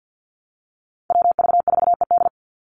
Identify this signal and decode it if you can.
Morse code, heard as a keyed tone, about 50 wpm, 700 Hz. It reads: WV4ED
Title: USA Call Signs 50wpm